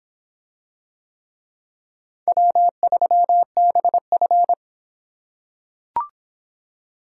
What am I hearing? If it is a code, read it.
Morse code, heard as a keyed tone, about 25 wpm, 700 Hz. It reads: W3BF